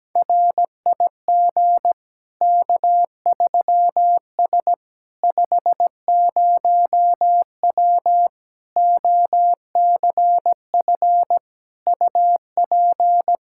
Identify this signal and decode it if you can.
Morse code, heard as a keyed tone, about 17 wpm, 700 Hz. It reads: RIG K3S 50W OCF UP
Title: Practice QSO - Exchange 2 - Info - 17wpm